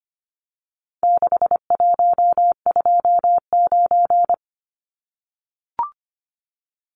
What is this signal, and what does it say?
Morse code, heard as a keyed tone, about 25 wpm, 700 Hz. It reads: 6129
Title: Numbers - 4-Digit 25wpm